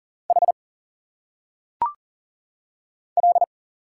Morse code, heard as a keyed tone, about 40 wpm, 700 Hz. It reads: H L
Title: Single Letters - Phonetic Pronunciation 40wpm